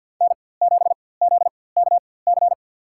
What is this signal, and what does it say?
Morse code, heard as a keyed tone, about 52 wpm, spaced as for 24 wpm, 700 Hz. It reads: N7ZKC
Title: USA Call Signs - 4x Character Spacing - 50wpm